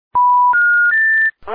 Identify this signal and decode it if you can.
Special information tone, with noise.